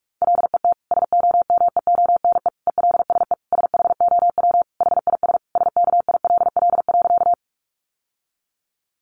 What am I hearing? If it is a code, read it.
Morse code, heard as a keyed tone, about 44 wpm, 700 Hz. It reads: LET SOMEONE ELSE SHOW HIS SKILL.
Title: Sentences from Top 1200 Words 45wpm